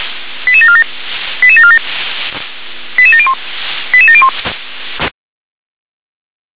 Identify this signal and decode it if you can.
Five-tone selective call (ZVEI), heard as quick runs of five tones, tones of about 70 ms. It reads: 88747 88747 88701 88701
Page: ZVEI